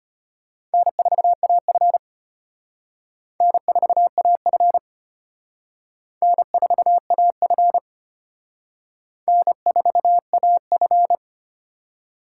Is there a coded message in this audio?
Yes: Morse code from 0.7 to 11.1 s, heard as a keyed tone, about 33 wpm, 700 Hz. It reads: N4AF N4AF N4AF N4AF